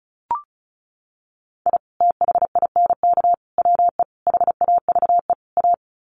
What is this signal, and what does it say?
Morse code, heard as a keyed tone, about 34 wpm, 700 Hz. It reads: I THINK WE HAVE A
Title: Sets of 5 Words Encore 35wpm